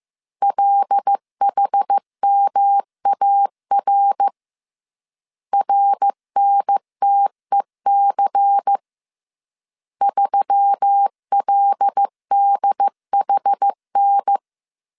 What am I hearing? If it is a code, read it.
Morse code, heard as a keyed tone, about 15 wpm, 800 Hz. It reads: LHMAR RNTEC 3LDHN
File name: CW / DET / 1 6 1500C S C 14wpm 2951